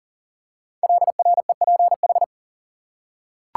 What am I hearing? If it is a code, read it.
Morse code, heard as a keyed tone, about 39 wpm, 700 Hz. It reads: LREPH
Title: Code Groups - 5 Letters - 40wpm